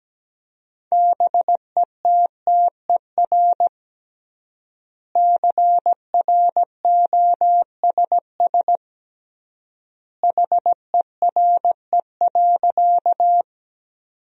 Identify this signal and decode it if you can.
Morse code, heard as a keyed tone, about 17 wpm, 700 Hz. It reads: BETTER CROSS HERE.
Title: 3x Word-Spacing - Sentences from Top 400 Words - 17wpm